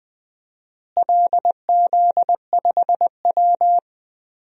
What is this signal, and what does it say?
Morse code, heard as a keyed tone, about 20 wpm, 700 Hz. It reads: LZ5W
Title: DXCC-100 Call Signs - 4 Characters - Repeated 2x - 20wpm